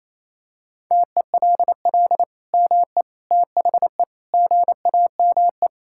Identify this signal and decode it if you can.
Morse code, heard as a keyed tone, about 28 wpm, 700 Hz. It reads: TELL ME THE GAME
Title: Sentences from Top 1600 Words 28wpm